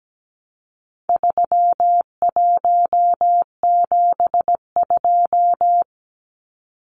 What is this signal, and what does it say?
Morse code, heard as a keyed tone, about 17 wpm, 700 Hz. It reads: 3172
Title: Numbers - 4-Digit 17wpm